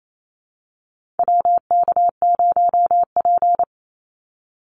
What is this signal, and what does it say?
Morse code, heard as a keyed tone, about 28 wpm, 700 Hz. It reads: WK0P
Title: Call Signs - 28wpm